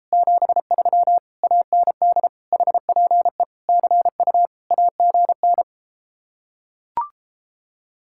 Morse code, heard as a keyed tone, about 33 wpm, 700 Hz. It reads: 73 AND HPE CU AGN